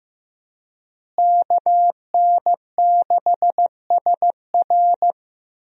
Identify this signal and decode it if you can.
Morse code, heard as a keyed tone, about 15 wpm, 700 Hz. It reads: KN6SR
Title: USA Call Signs - 5 Characters - Repeated 2x - 15wpm